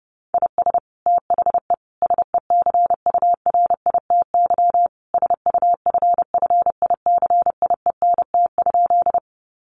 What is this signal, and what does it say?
Morse code, heard as a keyed tone, about 30 wpm, 700 Hz. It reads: IS THE SECURITY SUFFICIENT?
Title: Sentences from Top 1400 Words 30wpm